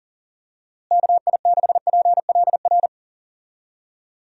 Morse code, heard as a keyed tone, about 40 wpm, 700 Hz. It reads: KI6PLR
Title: USA Call Signs - 6 Characters - Repeated 2x - 40wpm